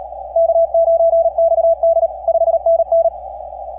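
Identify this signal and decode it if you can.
Morse code, heard as a keyed tone, about 34 wpm, 670 Hz. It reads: K9XD 5NN
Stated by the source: KP5/NP3VI on Desecheo heard in a 15 meter CW pileup.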